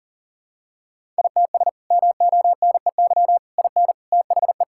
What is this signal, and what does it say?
Morse code, heard as a keyed tone, about 40 wpm, 700 Hz. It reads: ITS MONEY IN THE
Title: Sentences from Top 1000 Words - Review Part 6 - 40wpm